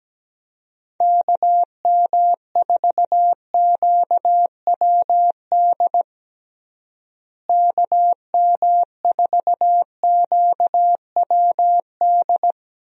Morse code, heard as a keyed tone, about 17 wpm, 700 Hz. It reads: KM4QWD KM4QWD
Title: USA Call Signs - 6 Characters - Repeated 3x - 17wpm